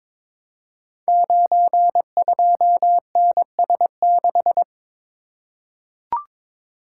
Morse code, heard as a keyed tone, about 22 wpm, 700 Hz. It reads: 92NS6
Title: Code Groups - 5 Letter:Numbers - 22wpm